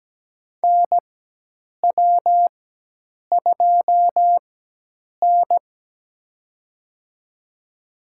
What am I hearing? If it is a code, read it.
Morse code, heard as a keyed tone, about 17 wpm, spaced as for 8 wpm, 700 Hz. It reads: NW2N
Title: USA Call Signs - 4 Characters - 4x Character Spacing - 17wpm